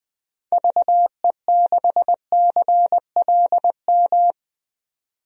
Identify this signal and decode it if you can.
Morse code, heard as a keyed tone, about 21 wpm, 700 Hz. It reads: VE6CLM
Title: DXCC-100 Call Signs - 6 Characters - 20wpm